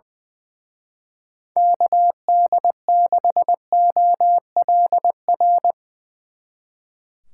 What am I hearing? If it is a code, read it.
Morse code, heard as a keyed tone, about 22 wpm, 700 Hz. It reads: KD6OLR
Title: Speed Racing - USA 2x3 Call Signs 20wpm